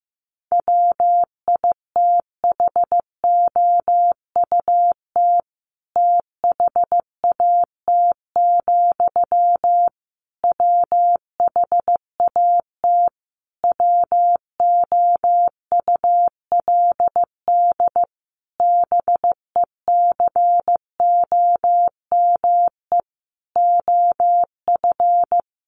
Morse code, heard as a keyed tone, about 15 wpm, 700 Hz. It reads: WITHOUT THAT, WHAT WOULD BECOME OF
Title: Sentences from Top 600 Words 15wpm